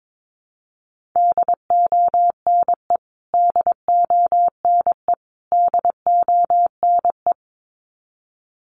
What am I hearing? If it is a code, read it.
Morse code, heard as a keyed tone, about 22 wpm, 700 Hz. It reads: DONE DONE DONE
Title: Top 500 Words Repeated 22wpm